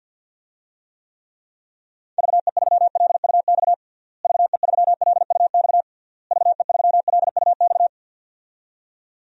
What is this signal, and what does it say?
Morse code, heard as a keyed tone, about 51 wpm, 700 Hz. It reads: VE3LUX VE3LUX VE3LUX
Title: DXCC-100 Call Signs - 6 Characters - Repeated 3x - 50wpm